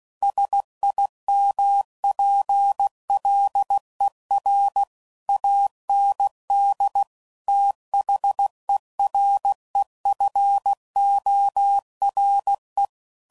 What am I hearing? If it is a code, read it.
Morse code, heard as a keyed tone, about 16 wpm, 800 Hz. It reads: SIMPLER AND THEREFORE
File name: CW Code Groups @15WPM / Technical Subjects / AND Gate